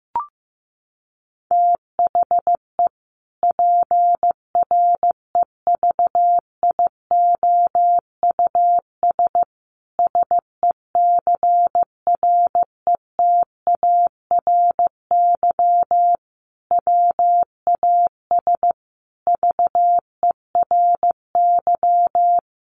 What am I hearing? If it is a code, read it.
Morse code, heard as a keyed tone, about 15 wpm, 700 Hz. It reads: THE PREVIOUS SECRETARY WAS VERY
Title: Sentences from Top 1500 Words 15wpm